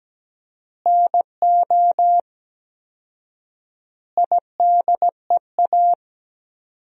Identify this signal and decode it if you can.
Morse code, heard as a keyed tone, about 17 wpm, 700 Hz. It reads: NO IDEA
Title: 4x Word-Spacing - Sentences from Top 300 Words - 17wpm